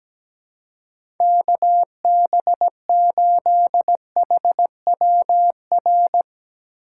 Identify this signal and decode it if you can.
Morse code, heard as a keyed tone, about 17 wpm, 700 Hz. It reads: KB8HWR
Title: USA Call Signs - 6 Characters - 3x Character Spacing - 17wpm